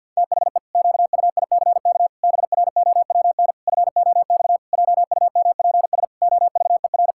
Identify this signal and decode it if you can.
Morse code, heard as a keyed tone, about 52 wpm, 700 Hz. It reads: THE QUICK BROWN FOX JUMPS OVER
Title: Pangram Challenge v2 - 50wpm